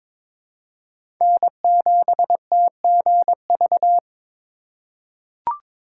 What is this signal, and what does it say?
Morse code, heard as a keyed tone, about 21 wpm, 700 Hz. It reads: N7TGV
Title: USA Call Signs - 3x Character Spacing - 22wpm